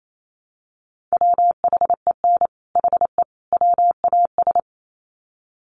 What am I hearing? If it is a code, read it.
Morse code, heard as a keyed tone, about 28 wpm, 700 Hz. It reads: WHEN HE WAS
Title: Sets of 3 Words 28wpm